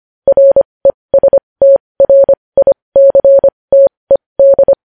Morse code, heard as a keyed tone, about 25 wpm, spaced as for 20 wpm, 550 Hz. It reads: RESTRICTED
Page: News Headlines in Morse Code at 20 WPM